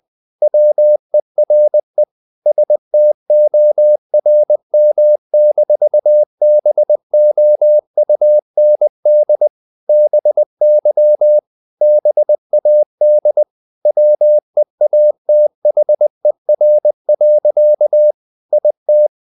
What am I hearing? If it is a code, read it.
Morse code, heard as a keyed tone, about 20 wpm, 600 Hz. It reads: WERE STORM-BOUND BY BAD WEATHER. IT
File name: the old man and the sea / the old man and the sea part 1 20wpm